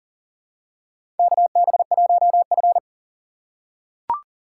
Morse code, heard as a keyed tone, about 39 wpm, 700 Hz. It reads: KB1F
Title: USA Call Signs - 4 Characters - 2x Character Spacing - 40wpm